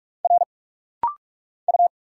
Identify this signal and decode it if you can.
Morse code, heard as a keyed tone, about 44 wpm, 700 Hz. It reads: R U
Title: Single Letter-Number Alt - Rapid-Fire 45wpm